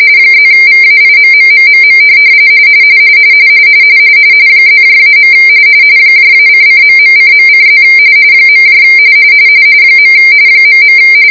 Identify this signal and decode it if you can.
RTTY, about 45.45 baud. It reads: +VAIMMMMM +-. -::..:.:+ . ..